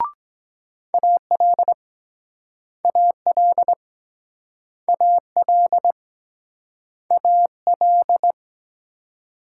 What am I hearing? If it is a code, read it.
Morse code, heard as a keyed tone, about 23 wpm, 700 Hz. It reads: AL AL AL AL